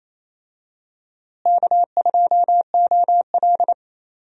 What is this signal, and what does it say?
Morse code, heard as a keyed tone, about 27 wpm, 700 Hz. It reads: K2OL
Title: USA Call Signs - 4 Characters - Repeated 3x - 28wpm